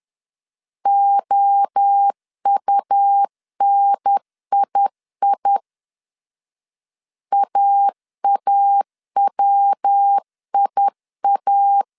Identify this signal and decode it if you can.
Morse code, heard as a keyed tone, about 10 wpm, 800 Hz. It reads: OUNII AAWIA
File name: CW / DET / 1 3 1500C S C 10wpm 3335